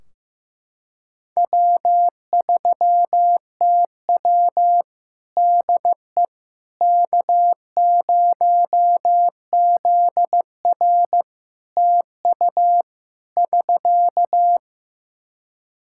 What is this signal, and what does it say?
Morse code, heard as a keyed tone, about 15 wpm, 700 Hz. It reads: W3TW DE K0ZR TU <SK>
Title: K1USN SST QSO Practice 15wpm